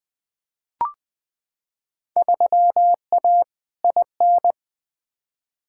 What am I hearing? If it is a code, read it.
Morse code, heard as a keyed tone, about 20 wpm, 700 Hz. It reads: 3A IN
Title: Field Day Exchanges - 20wpm